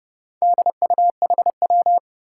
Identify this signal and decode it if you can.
Morse code, heard as a keyed tone, about 30 wpm, 700 Hz. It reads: DUHW
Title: Four Challenging Letter-Number Combinations 30wpm